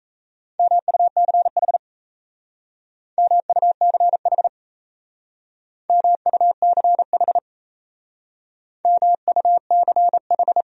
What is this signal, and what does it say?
Morse code, heard as a keyed tone, about 34 wpm, 700 Hz. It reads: MUCH MUCH MUCH MUCH
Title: Speed-Racing Top 200 Words 28wpm